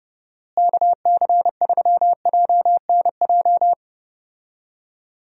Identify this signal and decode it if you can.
Morse code, heard as a keyed tone, about 30 wpm, 700 Hz. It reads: KC3JNJ
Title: USA Call Signs - 6 Characters - Repeated 2x - 30wpm